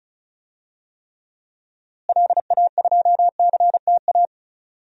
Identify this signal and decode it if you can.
Morse code, heard as a keyed tone, about 34 wpm, 700 Hz. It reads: LA2CTA
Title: DXCC-100 Call Signs - 6 Characters - Repeated 2x - 35wpm